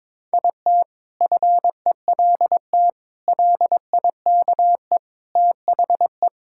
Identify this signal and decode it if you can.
Morse code, heard as a keyed tone, about 22 wpm, 700 Hz. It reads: IT FELT LIKE THE
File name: practice / 22wpm / Sentences with 8-Word Phrases - 22wpm